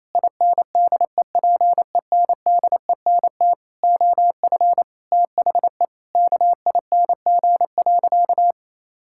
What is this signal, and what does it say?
Morse code, heard as a keyed tone, about 28 wpm, 700 Hz. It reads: INDEPENDENT OF THE KING.
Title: Sentences from Top 1000 Words - Review Part 6 - 28wpm